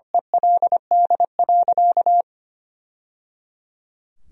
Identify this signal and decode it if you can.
Morse code, heard as a keyed tone, about 25 wpm, 700 Hz. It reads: ELD.
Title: Sentences from Top 1000 Words - Review Part 2 - 25wpm